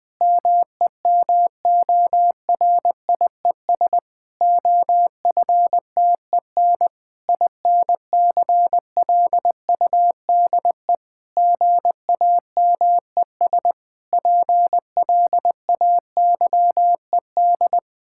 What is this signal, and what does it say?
Morse code, heard as a keyed tone, about 20 wpm, 700 Hz. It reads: MEMORIES OFTEN INCLUDE GAMES PLAYED
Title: Difficult Sentences with Hints - 20wpm